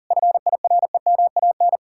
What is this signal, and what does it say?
Morse code, heard as a keyed tone, about 47 wpm, 700 Hz. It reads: FIREMAN
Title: Speed-Racing Common QSO Phrases - 40wpm